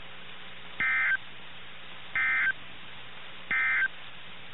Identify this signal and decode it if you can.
Emergency Alert System (SAME) tones, heard as harsh buzzing bursts. It is heard from 0.7 to 4.0 s.
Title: NOAA Weather Radio - KIH61